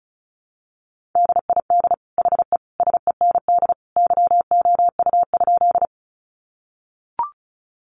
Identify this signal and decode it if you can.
Morse code, heard as a keyed tone, about 34 wpm, 700 Hz. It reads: DID HE SEND YOU?
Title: Sentences from Top 1500 Words 35wpm